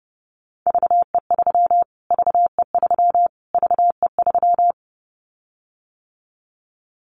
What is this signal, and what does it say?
Morse code, heard as a keyed tone, about 28 wpm, 700 Hz. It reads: VE3 VE3 VE3
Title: Callsign Prefixes for Top 25 DXCC - Repeated 3x - 30wpm